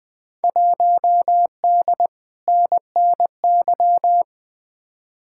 Morse code, heard as a keyed tone, about 21 wpm, 700 Hz. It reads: 1D NNY
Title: Field Day Exchanges - 20wpm